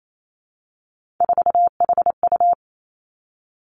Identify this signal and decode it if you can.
Morse code, heard as a keyed tone, about 28 wpm, 700 Hz. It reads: 4HU
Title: Three Challenging Letter-Number Combinations 28wpm